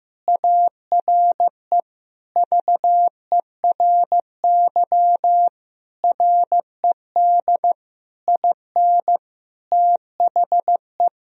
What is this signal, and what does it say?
Morse code, heard as a keyed tone, about 15 wpm, 700 Hz. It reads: ARE VERY RED IN THE
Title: Sentences from Top 1000 Words - Review Part 2 - 15wpm